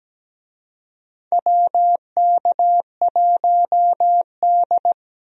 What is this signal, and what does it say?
Morse code, heard as a keyed tone, about 17 wpm, 700 Hz. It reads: WK1D
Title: USA Call Signs - 4 Characters - Repeated 2x - 17wpm